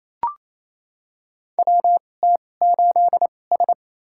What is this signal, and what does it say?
Morse code, heard as a keyed tone, about 28 wpm, spaced as for 20 wpm, 700 Hz. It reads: WT8S
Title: USA Call Signs - 4 Characters - 2x Character Spacing - 28wpm